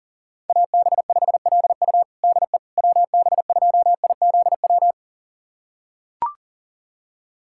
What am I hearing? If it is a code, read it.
Morse code, heard as a keyed tone, about 40 wpm, 700 Hz. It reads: AB5LU DE WB2IZW
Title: USA Callsign DE Callsign 40wpm